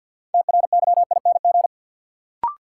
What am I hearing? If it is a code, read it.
Morse code, heard as a keyed tone, about 53 wpm, 700 Hz. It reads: TRYING
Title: Sets of 2 Words - New Word Drill - 50wpm